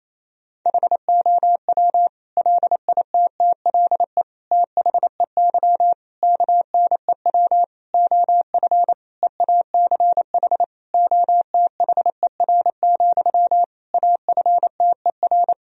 Morse code, heard as a keyed tone, about 28 wpm, 700 Hz. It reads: HOW LITTLE THEY KNEW OF EACH OTHER, AFTER
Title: Sentences from Top 1000 Words - Review Part 1 - 28wpm